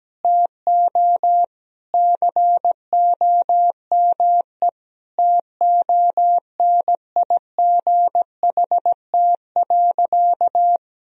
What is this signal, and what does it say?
Morse code, heard as a keyed tone, about 17 wpm, 700 Hz. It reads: TO COME TONIGHT.
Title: Sentences from Top 1000 Words - Review Part 3 - 17wpm